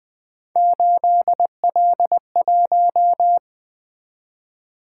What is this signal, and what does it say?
Morse code, heard as a keyed tone, about 20 wpm, 700 Hz. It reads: 8L1
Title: Three Letter-Number Combinations - 20wpm